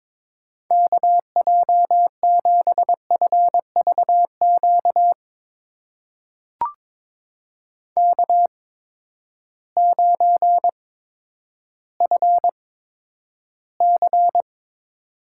Morse code, heard as a keyed tone, about 21 wpm, spaced as for 11 wpm, 700 Hz. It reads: KJ7FVQ K9FC
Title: USA Call Signs - 8x Character Spacing - 22wpm